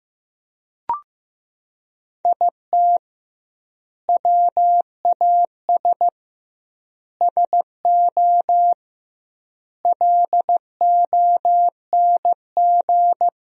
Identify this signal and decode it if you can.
Morse code, heard as a keyed tone, about 16 wpm, 700 Hz. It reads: IT WAS SO LONG
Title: 2x Word-Spacing - Sentences from Top 500 Words - 15wpm